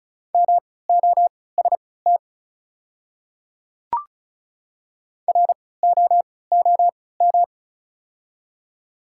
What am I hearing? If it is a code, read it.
Morse code, heard as a keyed tone, about 36 wpm, spaced as for 20 wpm, 690 Hz. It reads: MOST ROOM
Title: 3x Character Spacing - Top 300 Words - 35wpm